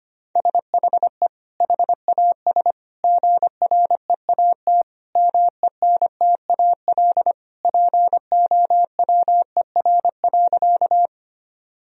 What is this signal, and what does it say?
Morse code, heard as a keyed tone, about 25 wpm, 700 Hz. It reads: SHE HAS GREAT MENTAL POWER.
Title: Sentences from Top 1000 Words - Review Part 4 - 25wpm